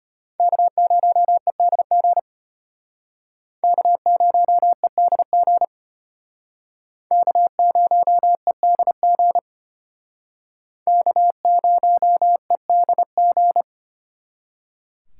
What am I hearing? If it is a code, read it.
Morse code, heard as a keyed tone, about 34 wpm, 700 Hz. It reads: K0EDG K0EDG K0EDG K0EDG